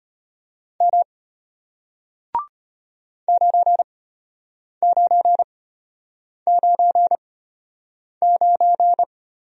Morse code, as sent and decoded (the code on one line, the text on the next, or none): -- / ----. ----. ----. ----.
M 9999